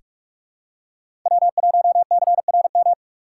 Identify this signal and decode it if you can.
Morse code, heard as a keyed tone, about 43 wpm, 700 Hz. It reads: W1CRM
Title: USA Call Signs - 2x Character Spacing - 45wpm